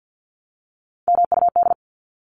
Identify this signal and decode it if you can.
Morse code, heard as a keyed tone, about 50 wpm, 700 Hz. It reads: MUD
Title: Top 1751-2000 Words 50wpm